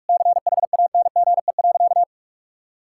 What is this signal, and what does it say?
Morse code, heard as a keyed tone, about 41 wpm, 700 Hz. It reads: CHANGE.
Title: Sentences from Top 1000 Words - Review Part 5 - 45wpm